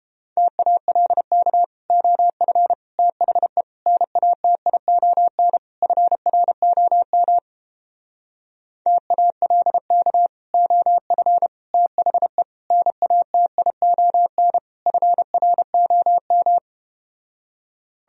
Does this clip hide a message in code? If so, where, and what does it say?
Yes: Morse code from 0.4 to 16.6 s, heard as a keyed tone, about 30 wpm, 700 Hz. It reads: TALK OF THE NATION FROM TALK OF THE NATION FROM